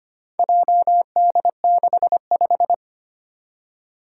Morse code, heard as a keyed tone, about 25 wpm, 700 Hz. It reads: JD65
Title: Four Challenging Letter-Number Combinations 25wpm